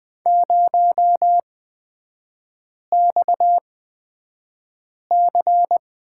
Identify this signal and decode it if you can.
Morse code, heard as a keyed tone, about 20 wpm, spaced as for 5 wpm, 700 Hz. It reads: 0XC
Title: 236 Lesson 38e - Letter q - Three Characters 20wpm